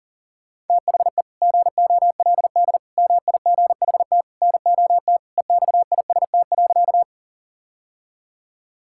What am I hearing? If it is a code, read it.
Morse code, heard as a keyed tone, about 40 wpm, 690 Hz. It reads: THE GOLD MIGHT NOT EXIST.
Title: Sentences from Top 1000 Words - Review Part 5 - 40wpm